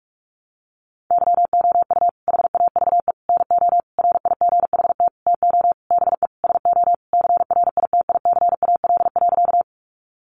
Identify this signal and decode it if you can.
Morse code, heard as a keyed tone, about 44 wpm, 700 Hz. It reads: YOU HAVE NO RIGHT TO BE SO CRITICAL.
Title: Sentences from Top 1100 Words 45wpm